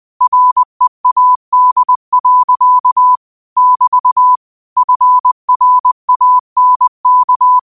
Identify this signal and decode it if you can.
Morse code, heard as a keyed tone, about 20 wpm, 1000 Hz. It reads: READ. <BT> FRANK